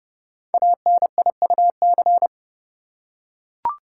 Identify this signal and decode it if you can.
Morse code, heard as a keyed tone, about 30 wpm, 700 Hz. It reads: ANIUC
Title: Code Groups - 5 Letter:Numbers - 30wpm